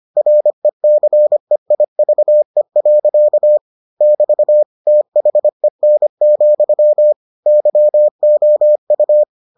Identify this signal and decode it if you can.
Morse code, heard as a keyed tone, about 25 wpm, 600 Hz. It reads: RECEIVE. <BT> THEN, YOU